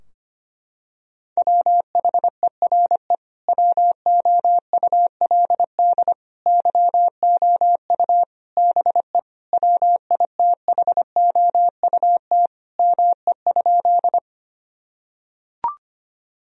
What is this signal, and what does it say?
Morse code, heard as a keyed tone, about 25 wpm, 700 Hz. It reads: WHERE WOULD YOU BE WITHOUT ME?
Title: Sentences from Top 1000 Words - Review Part 3 - 25wpm